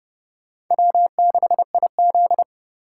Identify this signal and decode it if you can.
Morse code, heard as a keyed tone, about 30 wpm, 700 Hz. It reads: W6IZ
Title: USA Call Signs - 3x Character Spacing - 30wpm